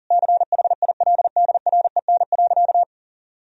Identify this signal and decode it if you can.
Morse code, heard as a keyed tone, about 40 wpm, 700 Hz. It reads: CHILDREN.
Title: Sentences from Top 1000 Words - Review Part 4 - 40wpm